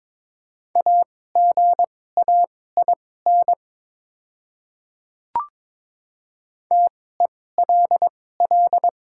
Morse code, heard as a keyed tone, about 22 wpm, spaced as for 16 wpm, 700 Hz. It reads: AGAIN TELL
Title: 2x Character Spacing - Top 200 Words - 22wpm